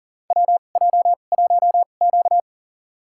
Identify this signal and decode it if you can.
Morse code, heard as a keyed tone, about 40 wpm, spaced as for 29 wpm, 700 Hz. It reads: WJ1Q
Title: USA Call Signs - 4 Characters - 2x Character Spacing - 40wpm